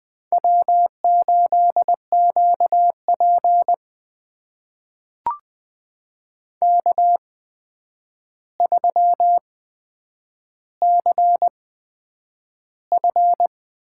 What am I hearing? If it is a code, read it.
Morse code, heard as a keyed tone, about 20 wpm, spaced as for 9 wpm, 700 Hz. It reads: W8QP K3CF
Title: USA Call Signs - 4 Characters - 8x Character Spacing - 20wpm